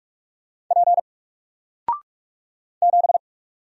Morse code, heard as a keyed tone, about 46 wpm, 700 Hz. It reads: P 7
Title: Single Letter-Number 45wpm